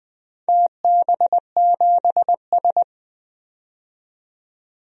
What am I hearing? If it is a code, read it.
Morse code, heard as a keyed tone, about 20 wpm, 700 Hz. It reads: TB7S
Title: DXCC-100 Call Signs - 4 Characters - 20wpm